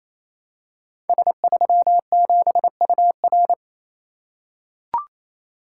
Morse code, heard as a keyed tone, about 27 wpm, 700 Hz. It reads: S37UR
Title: Code Groups - 5 Letter:Numbers - 28wpm